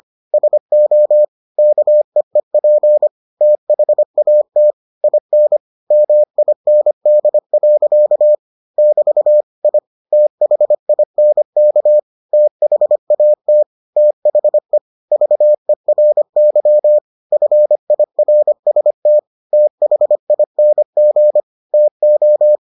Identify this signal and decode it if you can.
Morse code, heard as a keyed tone, about 25 wpm, 600 Hz. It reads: SO KEEP THAT IN MIND. <BT> I THINK THAT THE VERY FIRST THING TO